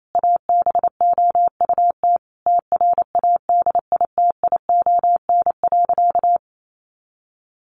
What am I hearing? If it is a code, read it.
Morse code, heard as a keyed tone, about 28 wpm, 700 Hz. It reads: ABOUT TRADITION.
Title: Sentences from Top 1500 Words 28wpm